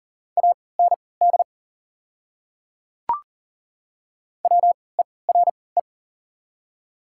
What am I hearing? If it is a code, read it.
Morse code, heard as a keyed tone, about 40 wpm, spaced as for 23 wpm, 690 Hz. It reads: AND WERE
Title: 3x Character Spacing - Top 100 Words - 40wpm